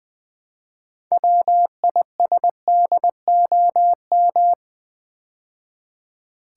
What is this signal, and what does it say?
Morse code, heard as a keyed tone, about 20 wpm, 700 Hz. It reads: WISDOM
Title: Build A Word - 3 Letters At A Time - 6-Letter Words - 20wpm